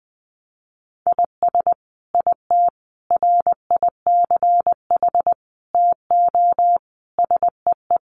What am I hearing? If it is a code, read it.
Morse code, heard as a keyed tone, about 20 wpm, 700 Hz. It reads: IS IT RICH TO SEE
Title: Sentences from Top 1100 Words 20wpm